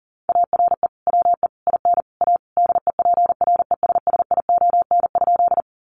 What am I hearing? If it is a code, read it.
Morse code, heard as a keyed tone, about 40 wpm, 700 Hz. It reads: ARE WE IN A DEPRESSION?
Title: Sentences from Top 1300 Words 40wpm